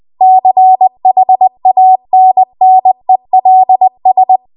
Morse code, heard as a keyed tone, about 20 wpm, 750 Hz. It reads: CHANNELS
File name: Archive / 20 WPM / 20WPM